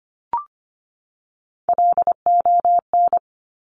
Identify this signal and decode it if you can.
Morse code, heard as a keyed tone, about 25 wpm, 700 Hz. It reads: LON
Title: Top Three Letter Combinations in Words 25wpm